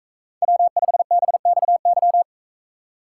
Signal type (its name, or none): Morse code